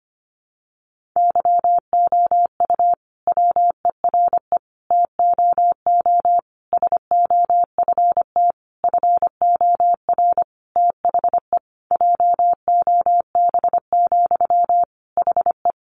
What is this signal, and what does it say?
Morse code, heard as a keyed tone, about 25 wpm, 700 Hz. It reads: YOU WERE TOO SOFT FOR THE JOB, HE
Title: Sentences from Top 1500 Words 25wpm